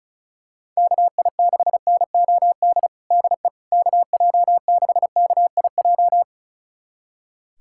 Morse code, heard as a keyed tone, about 36 wpm, 700 Hz. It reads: KI6NOD DE KJ6KIJ
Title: USA Callsign DE Callsign - 6 Characters - 35wpm